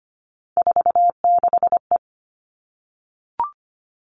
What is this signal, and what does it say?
Morse code, heard as a keyed tone, about 25 wpm, 700 Hz. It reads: 46E
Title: All Cut Numbers - 3-Digit 25wpm